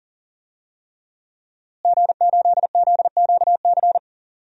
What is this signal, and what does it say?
Morse code, heard as a keyed tone, about 40 wpm, 700 Hz. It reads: G8ZQC
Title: DXCC-100 Call Signs - 5 Characters - Repeated 3x - 40wpm